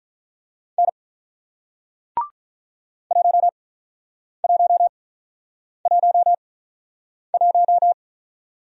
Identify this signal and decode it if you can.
Morse code, heard as a keyed tone, about 46 wpm, spaced as for 9 wpm, 700 Hz. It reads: N 1111